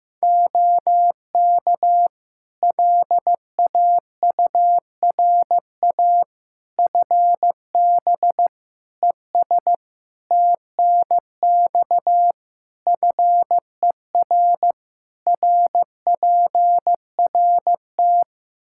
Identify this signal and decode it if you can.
Morse code, heard as a keyed tone, about 15 wpm, 700 Hz. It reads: OK LAURA FB ES TNX FER RPRT
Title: K1USN SST QSO Practice 15wpm